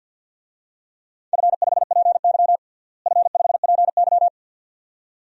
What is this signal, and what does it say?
Morse code, heard as a keyed tone, about 50 wpm, 700 Hz. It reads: F5PY F5PY
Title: DXCC-100 Call Signs - 4 Characters - Repeated 2x - 50wpm